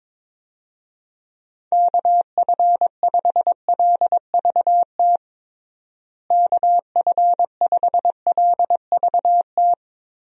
Morse code, heard as a keyed tone, about 22 wpm, 700 Hz. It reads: KF5LVT KF5LVT
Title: USA Call Signs - 6 Characters - Repeated 3x - 22wpm